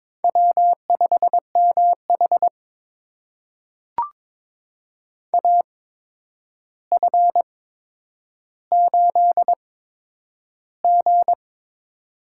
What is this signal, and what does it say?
Morse code, heard as a keyed tone, about 22 wpm, spaced as for 9 wpm, 700 Hz. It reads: W5MH AF8G
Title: USA Call Signs - 4 Characters - 8x Character Spacing - 22wpm